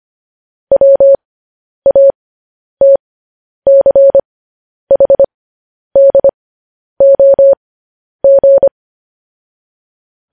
Morse code, heard as a keyed tone, about 25 wpm, spaced as for 10 wpm, 550 Hz. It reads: WATCHDOG
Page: news headlines in morse code at 10 wpm